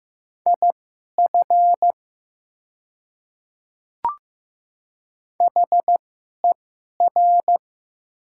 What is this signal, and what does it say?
Morse code, heard as a keyed tone, about 15 wpm, spaced as for 11 wpm, 700 Hz. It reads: IF HER
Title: 2x Character Spacing - Top 100 Words - 15wpm